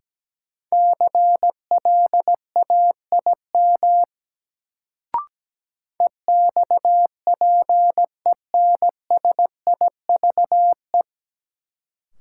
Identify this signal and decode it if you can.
Morse code, heard as a keyed tone, about 17 wpm, 700 Hz. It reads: CLAIM EXPENSIVE
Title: Sentences from Top 1500 Words - New Word Drill Alt - 17wpm